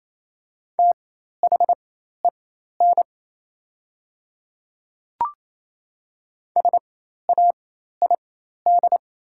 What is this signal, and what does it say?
Morse code, heard as a keyed tone, about 27 wpm, spaced as for 13 wpm, 700 Hz. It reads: THEN SAID
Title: 4x Character Spacing - Top 300 Words - 28wpm